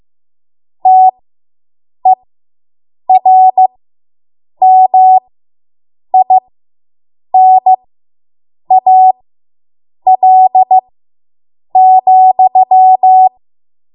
Morse code, heard as a keyed tone, about 15 wpm, spaced as for 7 wpm, 750 Hz. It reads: TERMINAL,